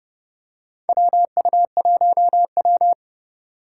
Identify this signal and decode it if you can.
Morse code, heard as a keyed tone, about 29 wpm, 700 Hz. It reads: WU1W
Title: USA Call Signs - 4 Characters - 2x Character Spacing - 30wpm